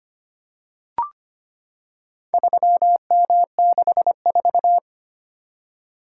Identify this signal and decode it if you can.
Morse code, heard as a keyed tone, about 25 wpm, 700 Hz. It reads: 3M64
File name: practice / 25wpm / Four Letter-Number Combinations - 25wpm